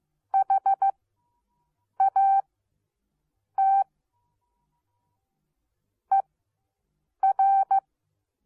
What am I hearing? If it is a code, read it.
Morse code, heard as a keyed tone, about 15 wpm, spaced as for 6 wpm, 780 Hz. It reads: HAT ER